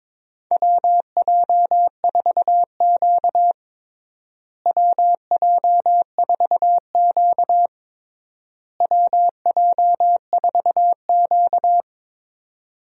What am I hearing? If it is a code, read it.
Morse code, heard as a keyed tone, about 22 wpm, 700 Hz. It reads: WJ4Q WJ4Q WJ4Q
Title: USA Call Signs - 4 Characters - Repeated 3x - 22wpm